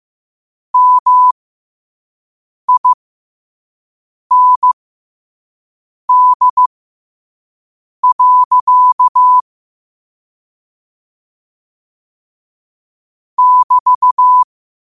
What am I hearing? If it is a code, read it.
Morse code, heard as a keyed tone, about 15 wpm, spaced as for 5 wpm, 1000 Hz. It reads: MIND. <BT>
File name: qotdfiles / 5wpm